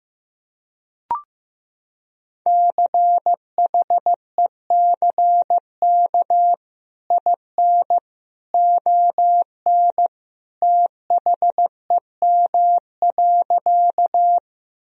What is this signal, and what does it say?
Morse code, heard as a keyed tone, about 16 wpm, 700 Hz. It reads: CHECK IN ON THEM.
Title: Sentences from Top 1000 Words - Review Part 3 - 15wpm